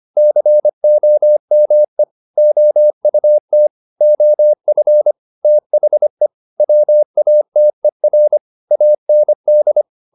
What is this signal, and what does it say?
Morse code, heard as a keyed tone, about 25 wpm, 600 Hz. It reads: COME OUT OF THE WATER AND